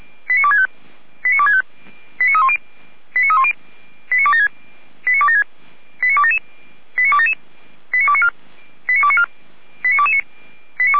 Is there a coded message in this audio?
Yes: five-tone selective call (ZVEI) from 0.3 to 10.2 s, heard as quick runs of five tones, tones of about 70 ms. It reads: 89265 89265 89310 89276 89276 89277 89277 89294 89294 89229